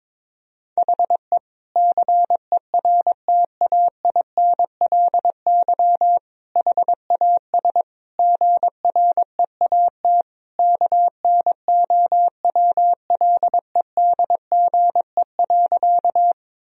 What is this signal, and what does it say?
Morse code, heard as a keyed tone, about 22 wpm, 700 Hz. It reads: HE CERTAINLY HAS GREAT KNOWLEDGE.
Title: Sentences from Top 1000 Words - Review Part 3 - 22wpm